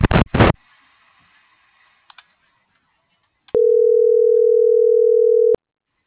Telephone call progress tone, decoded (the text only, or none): ringback tone